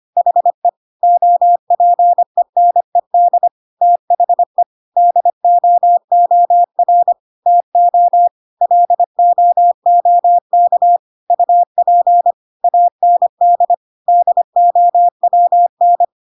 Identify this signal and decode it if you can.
Morse code, heard as a keyed tone, about 25 wpm, 700 Hz. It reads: HE OPENED THE DOOR TO LOOK UP AND DOWN
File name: Pinocchio / Pinocchio ch1 25 25 ik1hge